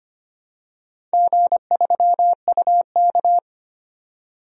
Morse code, heard as a keyed tone, about 25 wpm, 700 Hz. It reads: G3UK
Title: DXCC-100 Call Signs - 4 Characters - 25wpm